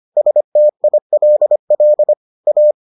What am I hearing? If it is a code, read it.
Morse code, heard as a keyed tone, about 25 wpm, 600 Hz. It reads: STILL A